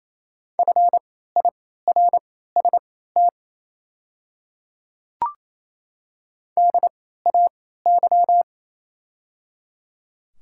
Morse code, heard as a keyed tone, about 28 wpm, spaced as for 16 wpm, 700 Hz. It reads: FIRST DAY
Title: 3x Character Spacing - Top 100 Words - 28wpm